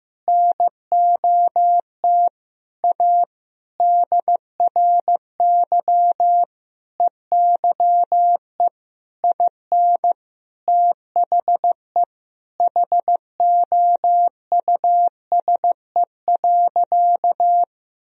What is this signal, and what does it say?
Morse code, heard as a keyed tone, about 15 wpm, 700 Hz. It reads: NOT A DRY EYE IN THE HOUSE.
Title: Sentences from Top 1000 Words - Review Part 3 - 15wpm